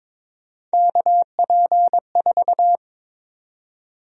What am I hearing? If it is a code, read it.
Morse code, heard as a keyed tone, about 23 wpm, 700 Hz. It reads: KP4
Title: USA Call Sign Prefixes - 22wpm